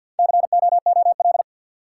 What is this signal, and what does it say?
Morse code, heard as a keyed tone, about 53 wpm, 700 Hz. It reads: COOL
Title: Sentences from Top 1200 Words - New Word Drill - 50wpm